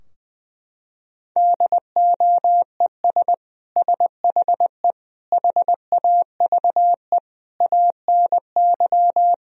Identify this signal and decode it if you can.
Morse code, heard as a keyed tone, about 21 wpm, 700 Hz. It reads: DOES SHE HAVE ANY
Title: Sentences from Top 1000 Words - Review Part 3 - 20wpm